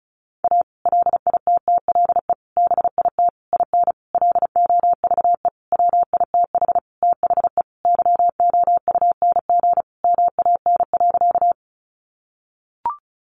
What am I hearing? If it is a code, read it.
Morse code, heard as a keyed tone, about 34 wpm, 700 Hz. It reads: A LITTLE BIT IN LOVE WITH THE YOUNG MAN.
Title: Sentences from Top 900 Words 35wpm